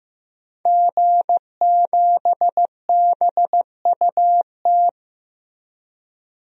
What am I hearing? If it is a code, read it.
Morse code, heard as a keyed tone, about 15 wpm, 700 Hz. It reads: G7BUT
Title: DXCC-100 Call Signs - 5 Characters - Repeated 3x - 15wpm